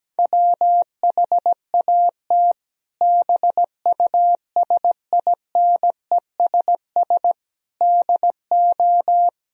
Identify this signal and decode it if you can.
Morse code, heard as a keyed tone, about 17 wpm, 700 Hz. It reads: WHAT BUSINESS DO
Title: Sentences from Top 1000 Words - Review Part 3 - 17wpm